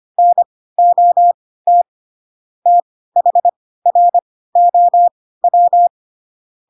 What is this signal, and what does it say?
Morse code, heard as a keyed tone, about 25 wpm, spaced as for 16 wpm, 700 Hz. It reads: NOT THROW
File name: Pinocchio / Pinocchio ch2 10 25 ik1hge